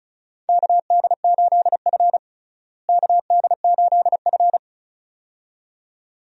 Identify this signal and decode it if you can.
Morse code, heard as a keyed tone, about 36 wpm, 700 Hz. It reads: KD8F KD8F
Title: USA Call Signs - 4 Characters - Repeated 2x - 35wpm